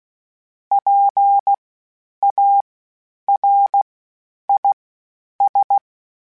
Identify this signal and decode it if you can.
Morse code, heard as a keyed tone, about 16 wpm, spaced as for 9 wpm, 800 Hz. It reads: PARIS
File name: PARIS Calibration / PARIS mp3s / PARIS@10-@15